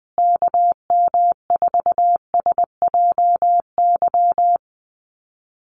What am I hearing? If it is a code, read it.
Morse code, heard as a keyed tone, about 20 wpm, 700 Hz. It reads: KM4SJY
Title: USA Call Signs Repeated 2x 20wpm